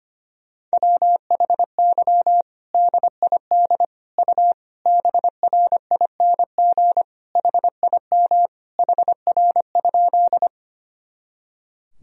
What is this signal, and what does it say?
Morse code, heard as a keyed tone, about 25 wpm, 700 Hz. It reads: WHY DID U BRING HIM HR?
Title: Sentences from Top 500 Words and Abbreviations 25wpm